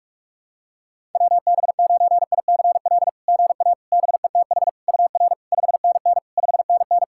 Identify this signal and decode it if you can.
Morse code, heard as a keyed tone, about 44 wpm, 660 Hz. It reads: WB9ICL GA BETH UR 5NN 5NN
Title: SOTA QSO Practice 45wpm